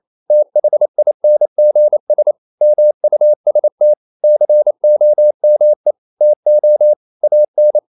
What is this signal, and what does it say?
Morse code, heard as a keyed tone, about 28 wpm, 600 Hz. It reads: THINGS MUST COME TO AN
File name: CW / Practice / EnglishSayings28